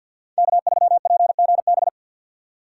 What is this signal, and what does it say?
Morse code, heard as a keyed tone, about 49 wpm, 700 Hz. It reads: K3PGB
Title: USA Call Signs - 5 Characters - 2x Character Spacing - 50wpm